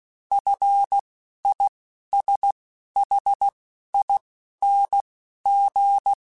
Morse code, heard as a keyed tone, about 16 wpm, spaced as for 11 wpm, 800 Hz. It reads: FISHING
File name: F and L / F and L mp3 / F and L Prefixes 3x2 Twice @10